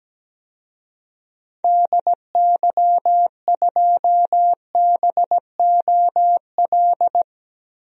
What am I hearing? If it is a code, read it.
Morse code, heard as a keyed tone, about 17 wpm, 700 Hz. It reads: DY2BOL
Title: DXCC-100 Call Signs - 6 Characters - 17wpm